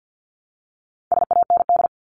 Morse code, heard as a keyed tone, about 50 wpm, 700 Hz. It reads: SAND
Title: Top 1501-1750 Words 50wpm